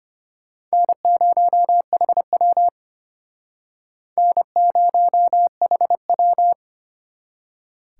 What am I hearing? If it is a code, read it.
Morse code, heard as a keyed tone, about 35 wpm, 700 Hz. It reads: N0HW N0HW